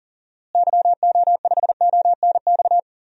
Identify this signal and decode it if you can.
Morse code, heard as a keyed tone, about 40 wpm, 700 Hz. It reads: YO5ONX
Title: DXCC-100 Call Signs - 6 Characters - Repeated 2x - 40wpm